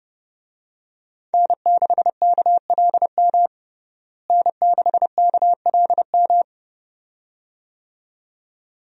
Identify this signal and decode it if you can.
Morse code, heard as a keyed tone, about 32 wpm, 700 Hz. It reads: N6KLM N6KLM
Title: USA Call Signs - 5 Characters - Repeated 2x - 30wpm